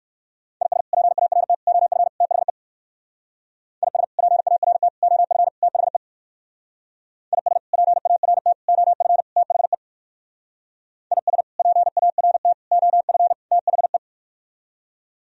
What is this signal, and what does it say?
Morse code, heard as a keyed tone, about 62 wpm, 700 Hz. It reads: IS PART OF THE IS PART OF THE IS PART OF THE IS PART OF THE